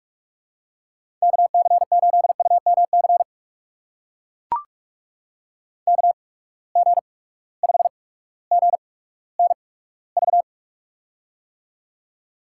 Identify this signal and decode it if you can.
Morse code, heard as a keyed tone, about 44 wpm, spaced as for 19 wpm, 700 Hz. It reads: KC8UMC KG5GNV
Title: USA Call Signs - 6 Characters - 8x Character Spacing - 45wpm